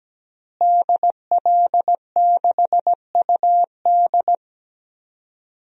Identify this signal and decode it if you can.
Morse code, heard as a keyed tone, about 17 wpm, 700 Hz. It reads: DL6UD
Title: DXCC-100 Call Signs - 5 Characters - Repeated 3x - 17wpm